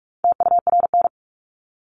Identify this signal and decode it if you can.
Morse code, heard as a keyed tone, about 44 wpm, 700 Hz. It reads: TURN
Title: Top 300 Words 45wpm